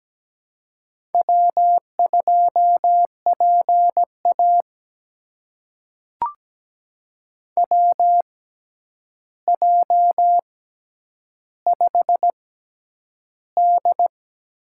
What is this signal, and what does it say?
Morse code, heard as a keyed tone, about 17 wpm, spaced as for 9 wpm, 700 Hz. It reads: W2PA WJ5D
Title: USA Call Signs - 4 Characters - 6x Character Spacing - 17wpm